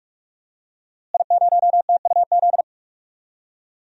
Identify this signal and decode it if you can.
Morse code, heard as a keyed tone, about 42 wpm, 700 Hz. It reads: I0TUZ
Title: DXCC-100 Call Signs - 5 Characters - Repeated 3x - 45wpm